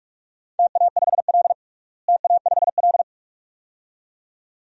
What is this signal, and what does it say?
Morse code, heard as a keyed tone, about 42 wpm, 700 Hz. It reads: TA5L TA5L
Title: DXCC-100 Call Signs - 4 Characters - Repeated 2x - 45wpm